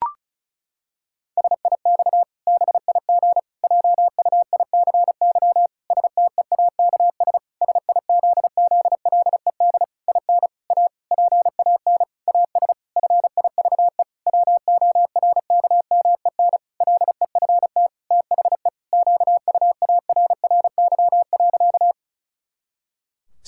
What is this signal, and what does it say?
Morse code, heard as a keyed tone, about 34 wpm, 700 Hz. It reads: SIX BIG JUICY STEAKS SIZZLED IN A PAN AS FIVE WORKMEN LEFT THE QUARRY.
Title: Pangram Challenge v2 - 35wpm